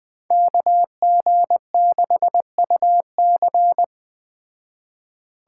Morse code, heard as a keyed tone, about 20 wpm, 700 Hz. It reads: KG6UC
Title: Speed Racing - USA 2x2 Call Signs 17wpm